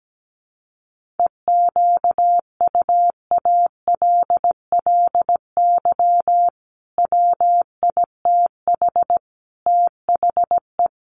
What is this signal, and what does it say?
Morse code, heard as a keyed tone, about 17 wpm, 700 Hz. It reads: EQUALLY WITH THE
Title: Sentences from Top 1400 Words 17wpm